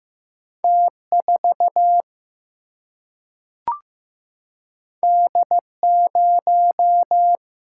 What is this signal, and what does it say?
Morse code, heard as a keyed tone, about 15 wpm, 700 Hz. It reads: T 4 D 0
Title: Single-Letter Single-Number 15wpm